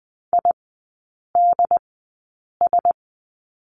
Morse code, heard as a keyed tone, about 20 wpm, spaced as for 8 wpm, 700 Hz. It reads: IDS
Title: 067 Lesson 12e - Letter d - Three Characters 20wpm